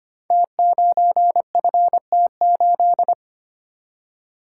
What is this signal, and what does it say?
Morse code, heard as a keyed tone, about 25 wpm, 700 Hz. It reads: T9FT8
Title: Code Groups - 5 Letter:Numbers - 25wpm